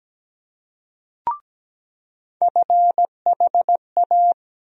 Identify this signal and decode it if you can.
Morse code, heard as a keyed tone, about 17 wpm, 700 Hz. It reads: FHA
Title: USA Call Sign Postfixes - 3 Letters 17wpm